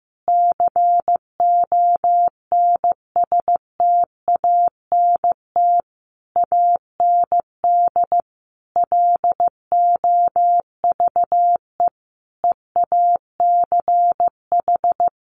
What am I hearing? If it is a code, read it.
Morse code, heard as a keyed tone, about 15 wpm, 700 Hz. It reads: CONSTANT AND LOVE EACH
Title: Sentences from Top 1500 Words 15wpm